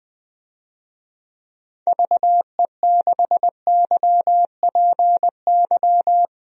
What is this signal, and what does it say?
Morse code, heard as a keyed tone, about 20 wpm, 700 Hz. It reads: VE6YPY
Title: DXCC-100 Call Signs - 6 Characters - Repeated 2x - 20wpm